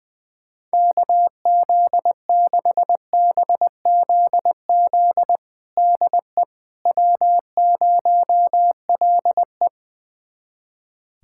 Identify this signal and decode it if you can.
Morse code, heard as a keyed tone, about 20 wpm, 700 Hz. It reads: KZ6BZZ DE W0LE
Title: USA Callsign DE Callsign 20wpm